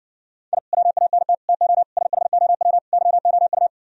Morse code, heard as a keyed tone, about 68 wpm, 700 Hz. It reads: I WANT TO SHOW YOU